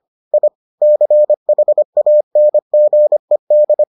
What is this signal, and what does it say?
Morse code, heard as a keyed tone, about 25 wpm, 600 Hz. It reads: I CHANGED